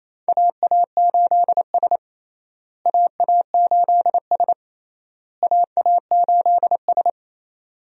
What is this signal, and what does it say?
Morse code, heard as a keyed tone, about 28 wpm, 700 Hz. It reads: AA8S AA8S AA8S
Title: USA Call Signs - 4 Characters - Repeated 3x - 28wpm